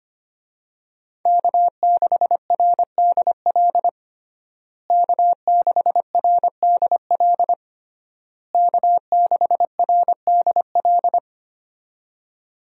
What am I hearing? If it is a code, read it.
Morse code, heard as a keyed tone, about 25 wpm, 700 Hz. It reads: K6RDL K6RDL K6RDL
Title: USA Call Signs - 5 Characters - Repeated 3x - 25wpm